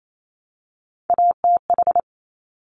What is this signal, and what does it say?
Morse code, heard as a keyed tone, about 28 wpm, 700 Hz. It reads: ATH
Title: Top Three Letter Combinations in Words 28wpm